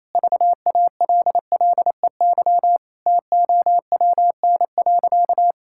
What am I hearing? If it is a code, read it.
Morse code, heard as a keyed tone, about 28 wpm, 700 Hz. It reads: VALLEY TOWN.
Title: Pangram Challenge v2 - 28wpm